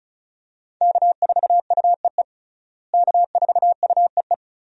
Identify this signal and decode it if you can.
Morse code, heard as a keyed tone, about 36 wpm, 700 Hz. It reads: K4UEE K4UEE
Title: USA Call Signs - 5 Characters - Repeated 3x - 35wpm